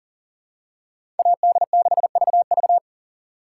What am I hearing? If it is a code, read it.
Morse code, heard as a keyed tone, about 39 wpm, 690 Hz. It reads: AD6VV
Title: USA Call Signs - 5 Characters - 6x Character Spacing - 40wpm